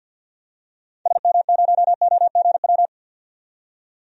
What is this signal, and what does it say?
Morse code, heard as a keyed tone, about 46 wpm, 700 Hz. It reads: SM0OGW
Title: DXCC-100 Call Signs - 6 Characters - 50wpm